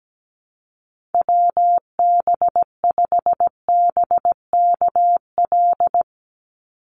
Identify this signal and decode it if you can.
Morse code, heard as a keyed tone, about 17 wpm, 700 Hz. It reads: WB5BKL
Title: NA SOTA Chaser Callsigns 17wpm